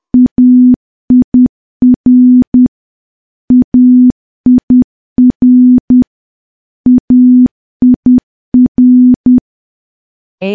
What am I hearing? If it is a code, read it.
Morse code, heard as a keyed tone, about 10 wpm, 250 Hz. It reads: AIR AIR AIR
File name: morse / morse-training-10WPM